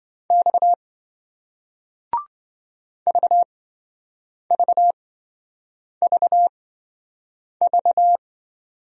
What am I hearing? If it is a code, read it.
Morse code, heard as a keyed tone, about 28 wpm, spaced as for 7 wpm, 700 Hz. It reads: X VVVV